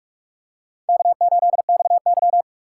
Morse code, heard as a keyed tone, about 44 wpm, 700 Hz. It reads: K8XY